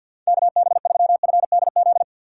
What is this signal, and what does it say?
Morse code, heard as a keyed tone, about 47 wpm, 700 Hz. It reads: KB3FDZ
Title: USA Call Signs - 6 Characters - 6x Character Spacing - 50wpm